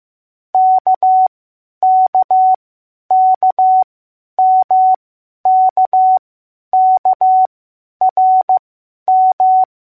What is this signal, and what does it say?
Morse code, heard as a keyed tone, about 15 wpm, spaced as for 10 wpm, 750 Hz. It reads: KKKMKKRM